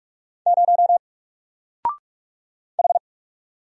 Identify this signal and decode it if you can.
Morse code, heard as a keyed tone, about 44 wpm, 700 Hz. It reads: 0 H
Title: Single Letter-Number 45wpm